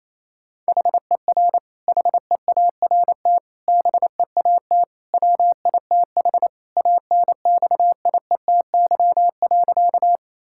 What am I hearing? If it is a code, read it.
Morse code, heard as a keyed tone, about 28 wpm, 700 Hz. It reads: HER HEART BEAT WITH ANXIETY.
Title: Sentences from Top 1600 Words 28wpm